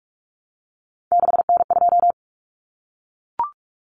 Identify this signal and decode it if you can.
Morse code, heard as a keyed tone, about 44 wpm, 700 Hz. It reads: 6N2
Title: Common Cut Numbers - 3-Digit 45wpm